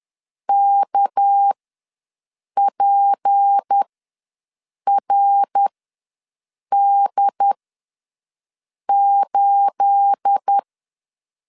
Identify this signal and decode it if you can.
Morse code, heard as a keyed tone, about 10 wpm, spaced as for 6 wpm, 800 Hz. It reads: KPRD8
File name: LEZIONI / 9 28 ws 10wpm 4124